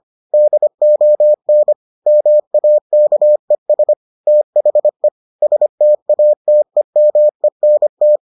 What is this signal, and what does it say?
Morse code, heard as a keyed tone, about 25 wpm, 600 Hz. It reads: DON MAKES THE STATEMENT